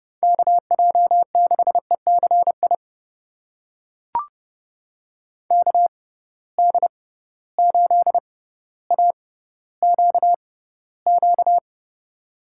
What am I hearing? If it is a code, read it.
Morse code, heard as a keyed tone, about 31 wpm, spaced as for 15 wpm, 700 Hz. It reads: KJ6ECI KD8AQQ
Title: USA Call Signs - 6 Characters - 6x Character Spacing - 30wpm